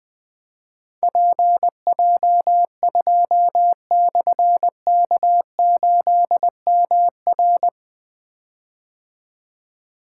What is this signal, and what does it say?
Morse code, heard as a keyed tone, about 21 wpm, 700 Hz. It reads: PJ2/K8MR
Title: Difficult Call Signs 20wpm